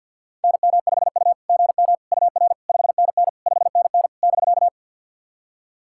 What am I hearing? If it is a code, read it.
Morse code, heard as a keyed tone, about 52 wpm, 690 Hz. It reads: NM5U GM UR 5NN 5NN <BK>
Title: SOTA QSO Practice 50wpm